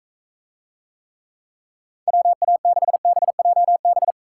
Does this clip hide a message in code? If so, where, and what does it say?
Yes: Morse code from 2.1 to 4.1 s, heard as a keyed tone, about 42 wpm, 700 Hz. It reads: WA6BJB